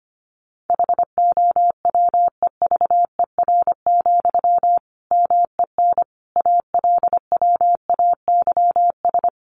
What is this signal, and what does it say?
Morse code, heard as a keyed tone, about 25 wpm, 700 Hz. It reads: HOWEVER, MEN ALWAYS
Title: Sentences from Top 600 Words 25wpm